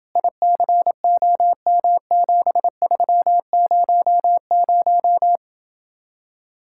Morse code, heard as a keyed tone, about 28 wpm, 700 Hz. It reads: ICOM7300